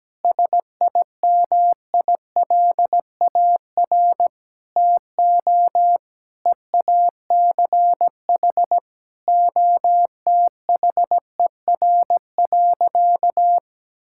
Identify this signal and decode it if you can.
Morse code, heard as a keyed tone, about 17 wpm, 700 Hz. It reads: SIMILAR TO EACH OTHER.
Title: Sentences from Top 1000 Words - Review Part 5 - 17wpm